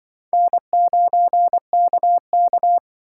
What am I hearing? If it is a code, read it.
Morse code, heard as a keyed tone, about 24 wpm, 700 Hz. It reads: N9KK
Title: Speed Racing - USA 1x2 Call Signs 20wpm